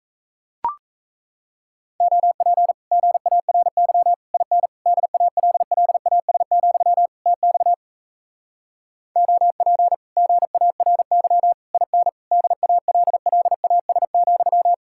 Morse code, heard as a keyed tone, about 37 wpm, 700 Hz. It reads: OP GARY IN DALLAS, TX OP GARY IN DALLAS,